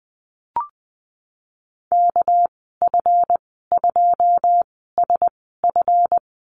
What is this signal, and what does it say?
Morse code, heard as a keyed tone, about 20 wpm, spaced as for 15 wpm, 700 Hz. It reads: KF2SF
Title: 207 Lesson 33g - Letter j - Call Signs 20wpm